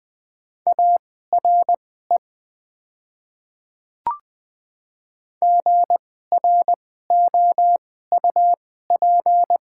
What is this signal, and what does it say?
Morse code, heard as a keyed tone, about 20 wpm, spaced as for 14 wpm, 700 Hz. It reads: ARE GROUP
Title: 2x Character Spacing - Top 300 Words - 20wpm